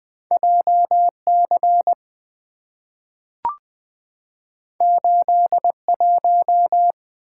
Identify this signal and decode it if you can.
Morse code, heard as a keyed tone, about 20 wpm, 700 Hz. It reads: JC 81
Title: Two Letter-Number Combinations - 20wpm